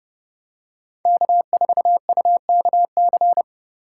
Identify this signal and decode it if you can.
Morse code, heard as a keyed tone, about 29 wpm, 700 Hz. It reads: K4UKC